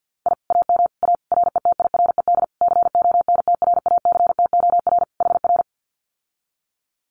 Morse code, heard as a keyed tone, about 50 wpm, 700 Hz. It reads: I AM A RETIRED CONTRACTOR HR
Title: Common QSO Phrases Encore 50wpm